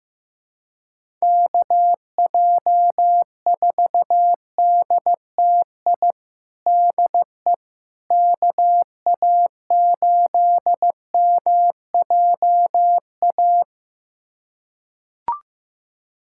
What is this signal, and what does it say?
Morse code, heard as a keyed tone, about 15 wpm, 700 Hz. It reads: KJ4DTI DE KA8MJA
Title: USA Callsign DE Callsign 15wpm